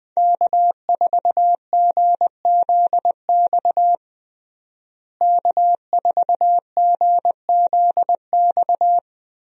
Morse code, heard as a keyed tone, about 20 wpm, 700 Hz. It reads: K4GZX K4GZX
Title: USA Call Signs - 5 Characters - Repeated 2x - 20wpm